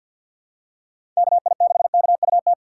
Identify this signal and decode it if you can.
Morse code, heard as a keyed tone, about 53 wpm, 700 Hz. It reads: KI6KUT
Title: USA Call Signs - 6 Characters - 3x Character Spacing - 50wpm